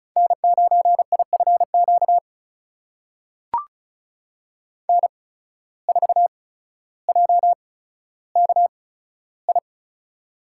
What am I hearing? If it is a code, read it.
Morse code, heard as a keyed tone, about 36 wpm, spaced as for 15 wpm, 700 Hz. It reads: N9IFQ N4JKI
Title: USA Call Signs - 5 Characters - 8x Character Spacing - 35wpm